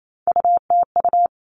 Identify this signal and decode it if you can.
Morse code, heard as a keyed tone, about 28 wpm, 700 Hz. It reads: UTU
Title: All Cut Numbers - 3-Digit 28wpm